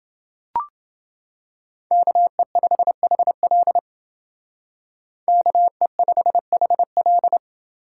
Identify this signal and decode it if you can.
Morse code, heard as a keyed tone, about 30 wpm, 700 Hz. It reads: KE5HL KE5HL